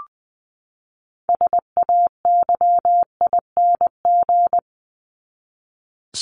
Morse code, heard as a keyed tone, about 20 wpm, 700 Hz. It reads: SAYING
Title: ING Suffix - Top 100 Words with Base Words 20wpm